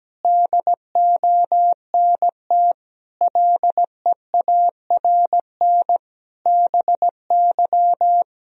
Morse code, heard as a keyed tone, about 17 wpm, 700 Hz. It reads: DONT LEARN BY
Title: Sentences from Top 1000 Words - Review Part 3 - 17wpm